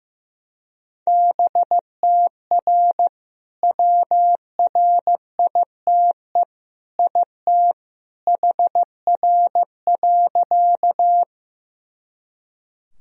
Morse code, heard as a keyed tone, about 15 wpm, 700 Hz. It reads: BTR WRITE IT HR.
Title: Sentences from Top 400 Words and Abbreviations 15wpm